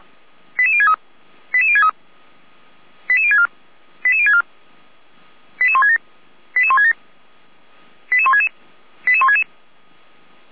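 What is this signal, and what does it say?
Five-tone selective call (ZVEI), heard as quick runs of five tones, tones of about 70 ms. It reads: 88063 88063 88064 88064 88167 88167 88166 88166
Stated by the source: ZVEI five-tone selective call 88352 received via SDR.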